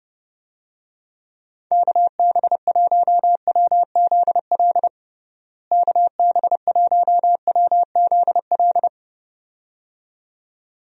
Morse code, heard as a keyed tone, about 32 wpm, 700 Hz. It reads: KB1WZL KB1WZL
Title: USA Call Signs - 6 Characters - Repeated 2x - 30wpm